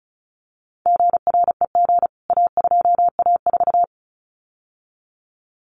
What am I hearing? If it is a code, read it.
Morse code, heard as a keyed tone, about 34 wpm, 700 Hz. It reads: GREG A2A4
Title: CWT Exchange 35wpm